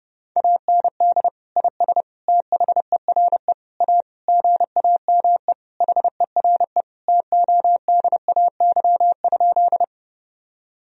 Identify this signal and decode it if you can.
Morse code, heard as a keyed tone, about 32 wpm, 700 Hz. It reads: AND IS THERE A GAME HERE TODAY?
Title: Sentences from Top 1000 Words - Review Part 3 - 30wpm